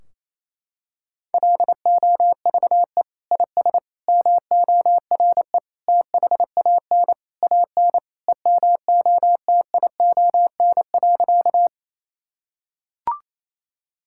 Morse code, heard as a keyed tone, about 28 wpm, 700 Hz. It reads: LOVE IS MORE THAN AN EMOTION.
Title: Sentences from Top 1600 Words 28wpm